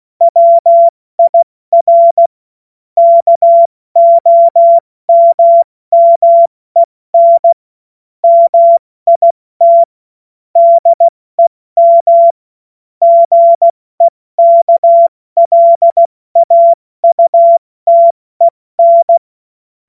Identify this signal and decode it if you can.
Morse code, heard as a keyed tone, about 16 wpm, 680 Hz. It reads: WIR KOMMEN MIT DEM GEKLAUTEN
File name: Morsekurs / Witze / Witz 26. Oktober 2021, 70 F80